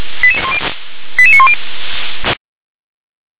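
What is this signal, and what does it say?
Five-tone selective call (ZVEI), heard as quick runs of five tones, tones of about 70 ms. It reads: 88910
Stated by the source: ZVEI five-tone selective call 88853, received on SDR.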